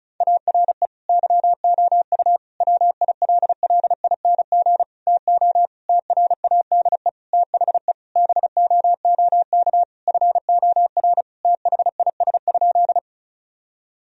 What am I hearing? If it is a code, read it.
Morse code, heard as a keyed tone, about 36 wpm, 700 Hz. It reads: ARE YOU WILLING TO TRADE THE BOOK FOR THIS?
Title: Sentences from Top 1000 Words - Review Part 4 - 35wpm